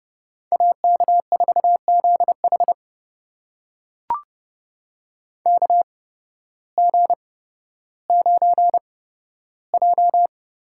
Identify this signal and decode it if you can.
Morse code, heard as a keyed tone, about 30 wpm, spaced as for 14 wpm, 700 Hz. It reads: AK4ZH KG9J
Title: USA Call Signs - 5 Characters - 8x Character Spacing - 30wpm